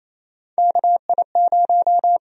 Morse code, heard as a keyed tone, about 28 wpm, 700 Hz. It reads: KI0
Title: USA Call Sign Prefixes - 28wpm